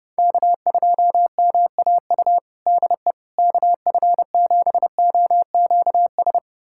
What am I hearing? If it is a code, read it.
Morse code, heard as a keyed tone, about 31 wpm, 700 Hz. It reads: K2MAU DE KF7OQS
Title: USA Callsign DE Callsign 30wpm